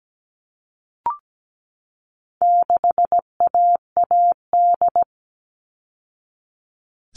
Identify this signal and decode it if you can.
Morse code, heard as a keyed tone, about 17 wpm, 700 Hz. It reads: 6AAD
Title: All Cut Numbers - 4-Digit 17wpm